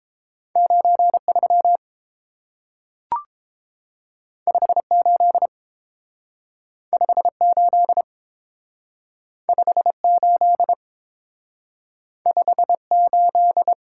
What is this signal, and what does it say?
Morse code, heard as a keyed tone, about 30 wpm, 700 Hz. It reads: 93 58 58 58 58